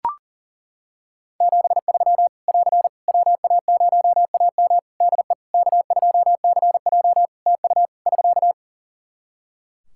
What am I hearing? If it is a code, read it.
Morse code, heard as a keyed tone, about 40 wpm, 700 Hz. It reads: 73 <AR> WA0AM DE K2CJ TU <SK>
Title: K1USN SST QSO Practice 40wpm